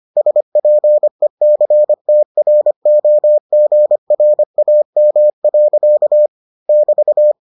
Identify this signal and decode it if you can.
Morse code, heard as a keyed tone, about 25 wpm, 600 Hz. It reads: SPECTROGRAM. <BT>